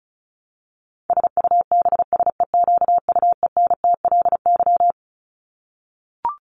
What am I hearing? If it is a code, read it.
Morse code, heard as a keyed tone, about 34 wpm, 700 Hz. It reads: SUBSEQUENTLY
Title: LY Suffix - Top 100 Words with Base Words 35wpm